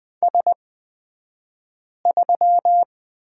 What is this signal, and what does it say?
Morse code, heard as a keyed tone, about 20 wpm, spaced as for 5 wpm, 700 Hz. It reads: S3
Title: 242 Lesson 39d - Letter z - Two Characters 20wpm